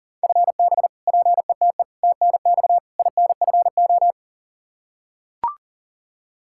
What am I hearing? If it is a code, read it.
Morse code, heard as a keyed tone, about 40 wpm, 700 Hz. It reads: FB PETE TNX INFO
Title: Practice QSO - Exchange 3 - More Info - 40wpm